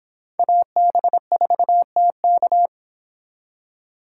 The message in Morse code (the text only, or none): AB4TK